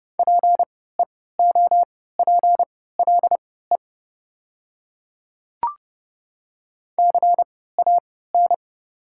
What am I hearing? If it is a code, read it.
Morse code, heard as a keyed tone, about 30 wpm, spaced as for 17 wpm, 690 Hz. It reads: PEOPLE CAN
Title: 3x Character Spacing - Top 100 Words - 30wpm